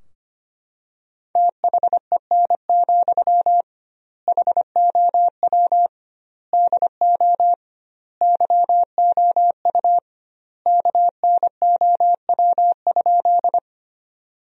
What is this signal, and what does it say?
Morse code, heard as a keyed tone, about 25 wpm, 700 Hz. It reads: THEN, HOW DO YOU KNOW?
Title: 2x Word-Spacing - Sentences from Top 100 Words - 25wpm